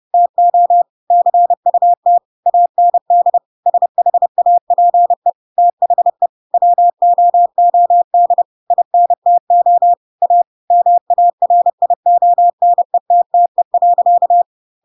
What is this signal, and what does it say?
Morse code, heard as a keyed tone, about 30 wpm, 700 Hz. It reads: TO CUT AND SHAPE THE WOOD INTO A MARIONETTE.
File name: Pinocchio / Pinocchio ch3 30 30 ik1hge